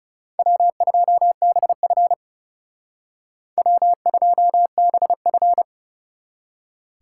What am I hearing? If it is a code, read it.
Morse code, heard as a keyed tone, about 38 wpm, 700 Hz. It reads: W2BF W2BF